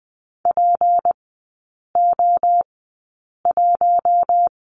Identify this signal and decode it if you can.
Morse code, heard as a keyed tone, about 20 wpm, spaced as for 8 wpm, 700 Hz. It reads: PO1
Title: 205 Lesson 33e - Letter j - Three Characters 20wpm